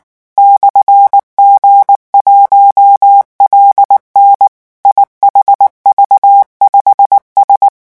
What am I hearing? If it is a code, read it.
Morse code, heard as a keyed tone, about 19 wpm, 800 Hz. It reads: /G1LN IHV5S
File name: morse mp3 / revision12-vitesse 18 mots